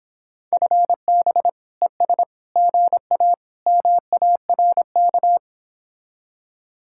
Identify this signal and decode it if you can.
Morse code, heard as a keyed tone, about 26 wpm, 700 Hz. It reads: FB ES GA MARK